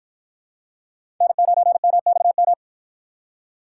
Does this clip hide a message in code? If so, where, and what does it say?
Yes: Morse code from 1.2 to 2.5 s, heard as a keyed tone, about 53 wpm, 700 Hz. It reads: N9MXM